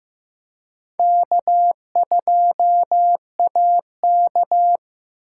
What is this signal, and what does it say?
Morse code, heard as a keyed tone, about 18 wpm, 700 Hz. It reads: K2AK
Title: Speed Racing - USA 1x2 Call Signs 15wpm